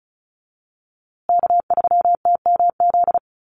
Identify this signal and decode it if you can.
Morse code, heard as a keyed tone, about 34 wpm, 700 Hz. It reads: K3TMZ
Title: USA Call Signs Repeated 2x 35wpm